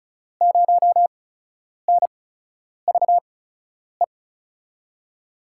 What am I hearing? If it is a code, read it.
Morse code, heard as a keyed tone, about 33 wpm, spaced as for 9 wpm, 700 Hz. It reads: 0NVE
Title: USA Call Signs - 6 Characters - 8x Character Spacing - 35wpm